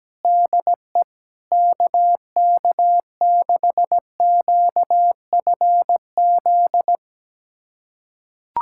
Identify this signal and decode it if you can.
Morse code, heard as a keyed tone, about 17 wpm, 700 Hz. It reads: DE KK6QFZ
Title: USA Callsign DE Callsign 17wpm